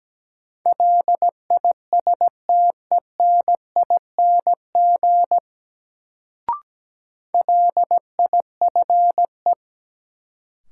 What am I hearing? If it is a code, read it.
Morse code, heard as a keyed tone, about 17 wpm, 700 Hz. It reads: LISTENING LIFE
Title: Sets of 5 Words - New Word Drill - 17wpm